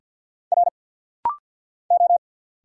Morse code, heard as a keyed tone, about 50 wpm, 690 Hz. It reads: R O
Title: Single Letters - Rapid-Fire 50wpm